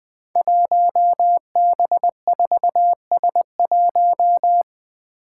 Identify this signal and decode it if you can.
Morse code, heard as a keyed tone, about 20 wpm, 700 Hz. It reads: 1B4S1
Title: Five Challenging Letter-Number Combinations 20wpm